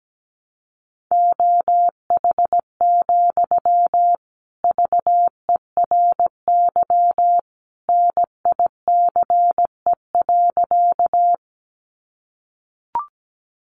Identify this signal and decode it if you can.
Morse code, heard as a keyed tone, about 17 wpm, 700 Hz. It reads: OH, VERY NICE.
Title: Sentences from Top 1100 Words 17wpm